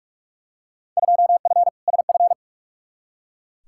Morse code, heard as a keyed tone, about 42 wpm, 700 Hz. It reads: 2F SF
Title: Field Day Exchanges - 45wpm